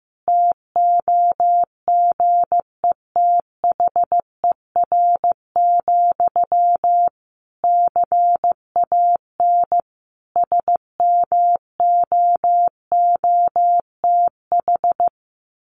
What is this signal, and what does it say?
Morse code, heard as a keyed tone, about 15 wpm, 700 Hz. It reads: TOGETHER, CAN SMOOTH
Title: Sentences from Top 1400 Words 15wpm